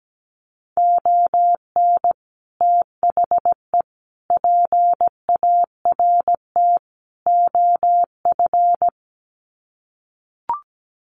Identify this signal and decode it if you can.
Morse code, heard as a keyed tone, about 17 wpm, 700 Hz. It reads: ON THE PART OF
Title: Sets of 4 Words 17wpm